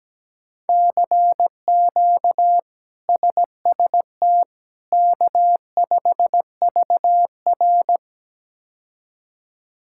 Morse code, heard as a keyed tone, about 17 wpm, 700 Hz. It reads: CQ SST K5VR
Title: K1USN SST QSO Practice - 17wpm